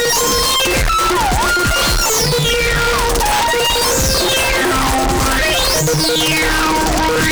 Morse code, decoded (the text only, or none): none